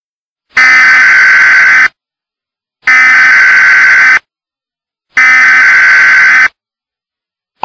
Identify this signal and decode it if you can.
Emergency Alert System (SAME) tones (header bursts), heard as harsh buzzing bursts. It is heard from 0.5 to 6.5 s.
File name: Eas Warning Of Civil Unrest And Or Zombies